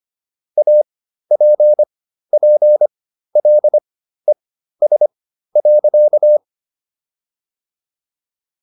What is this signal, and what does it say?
Morse code, heard as a keyed tone, about 25 wpm, spaced as for 13 wpm, 600 Hz. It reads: APPLES.